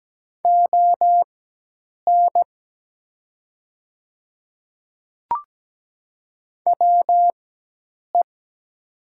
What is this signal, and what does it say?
Morse code, heard as a keyed tone, about 17 wpm, spaced as for 8 wpm, 690 Hz. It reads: ON WE
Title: 4x Character Spacing - Top 100 Words - 17wpm